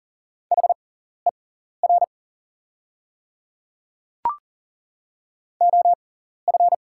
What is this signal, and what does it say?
Morse code, heard as a keyed tone, about 40 wpm, spaced as for 14 wpm, 690 Hz. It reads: HER OF
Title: 6x Character Spacing - Top 100 Words - 40wpm